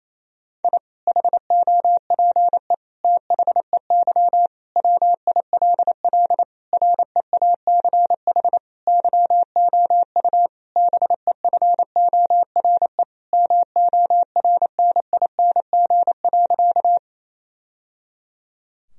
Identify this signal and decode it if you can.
Morse code, heard as a keyed tone, about 28 wpm, 700 Hz. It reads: I HOPE THEY WILL REACH YOU BEFORE MORNING.
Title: Sentences from Top 1000 Words - Review Part 2 - 28wpm